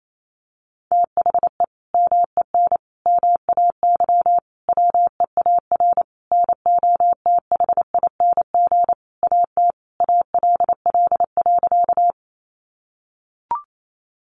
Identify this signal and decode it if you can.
Morse code, heard as a keyed tone, about 28 wpm, 700 Hz. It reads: THE MEN MAY WEAR NOTHING AT ALL.
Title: Sentences from Top 1500 Words 28wpm